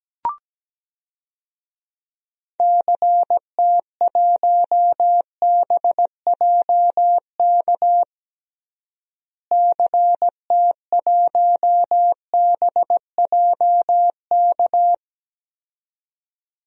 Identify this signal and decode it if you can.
Morse code, heard as a keyed tone, about 17 wpm, 700 Hz. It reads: CT1BJK CT1BJK
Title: DXCC-100 Call Signs - 6 Characters - Repeated 2x - 17wpm